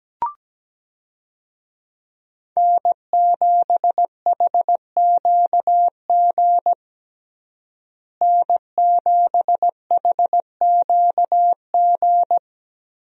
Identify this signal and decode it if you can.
Morse code, heard as a keyed tone, about 17 wpm, 700 Hz. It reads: N7HQG N7HQG
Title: USA Call Signs - 5 Characters - Repeated 3x - 17wpm